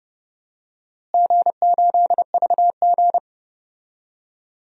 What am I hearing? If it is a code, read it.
Morse code, heard as a keyed tone, about 30 wpm, 700 Hz. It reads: G8VG
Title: DXCC-100 Call Signs - 4 Characters - Repeated 3x - 30wpm